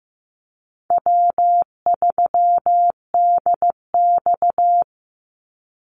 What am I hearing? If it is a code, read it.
Morse code, heard as a keyed tone, about 15 wpm, 700 Hz. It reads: W3DX
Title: NA SOTA Activator Callsigns 15wpm